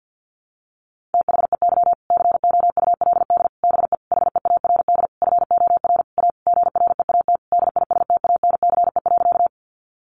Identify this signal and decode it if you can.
Morse code, heard as a keyed tone, about 50 wpm, 700 Hz. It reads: THEY COULD BE HEARD FOR A GREAT DISTANCE.
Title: Sentences from Top 1200 Words 50wpm